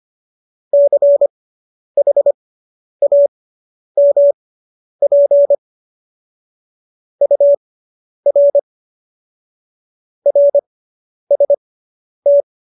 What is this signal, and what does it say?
Morse code, heard as a keyed tone, about 25 wpm, spaced as for 10 wpm, 570 Hz. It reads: CHAMP UR RST